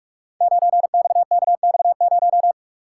Morse code, heard as a keyed tone, about 44 wpm, 700 Hz. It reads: 9XKX0
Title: Code Groups - 5 Letter:Numbers - 45wpm